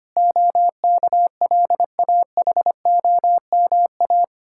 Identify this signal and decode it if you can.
Morse code, heard as a keyed tone, about 25 wpm, 700 Hz. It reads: OKLAHOMA
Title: Practice QSO - Exchange 1 - Beginning - 25wpm